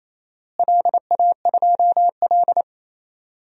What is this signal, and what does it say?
Morse code, heard as a keyed tone, about 28 wpm, 700 Hz. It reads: LA2L
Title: DXCC-100 Call Signs - 4 Characters - Repeated 2x - 28wpm